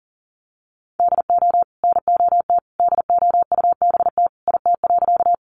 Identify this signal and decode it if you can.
Morse code, heard as a keyed tone, about 40 wpm, 700 Hz. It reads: DO NOT DOUBT IT.
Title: Sentences from Top 1500 Words 40wpm